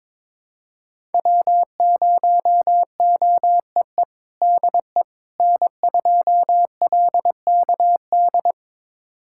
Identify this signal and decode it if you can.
Morse code, heard as a keyed tone, about 22 wpm, 700 Hz. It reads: W0OEE DE N2LKD
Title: USA Callsign DE Callsign - 5 Characters - 22wpm